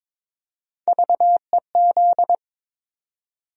Morse code, heard as a keyed tone, about 22 wpm, 700 Hz. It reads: VEZ
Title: USA Call Sign Postfixes - 3 Letters 22wpm